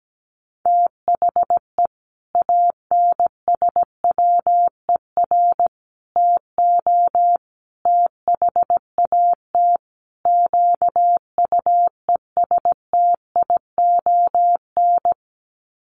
Morse code, heard as a keyed tone, about 17 wpm, 700 Hz. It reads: THE ANSWER TO THAT QUESTION
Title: Sets of 5 Words Encore2 17wpm